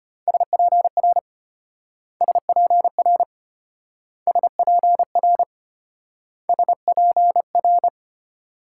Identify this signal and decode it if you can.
Morse code, heard as a keyed tone, about 34 wpm, 700 Hz. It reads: SPR SPR SPR SPR